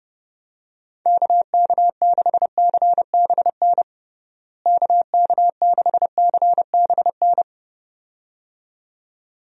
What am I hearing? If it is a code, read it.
Morse code, heard as a keyed tone, about 32 wpm, 700 Hz. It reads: KK6CBN KK6CBN
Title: USA Call Signs - 6 Characters - Repeated 2x - 30wpm